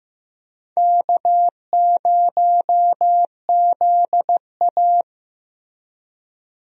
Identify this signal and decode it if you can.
Morse code, heard as a keyed tone, about 15 wpm, 700 Hz. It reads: K0ZA
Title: USA Call Signs - 4 Characters - Repeated 2x - 15wpm